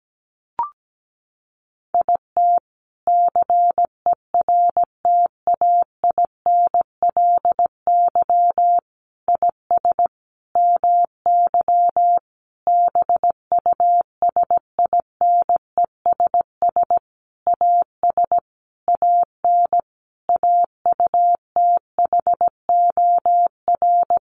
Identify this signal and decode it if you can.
Morse code, heard as a keyed tone, about 17 wpm, 700 Hz. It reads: IT CERTAINLY IS MY BUSINESS AS AN AUTHOR
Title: Sentences from Top 1200 Words 17wpm